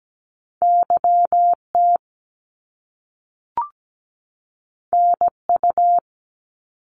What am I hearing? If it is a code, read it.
Morse code, heard as a keyed tone, about 17 wpm, 700 Hz. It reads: YT NU
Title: Canadian Provinces and Territories 17wpm